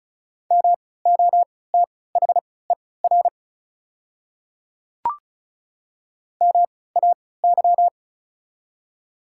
Morse code, heard as a keyed tone, about 36 wpm, spaced as for 20 wpm, 700 Hz. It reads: MOTHER MAY
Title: 3x Character Spacing - Top 200 Words - 35wpm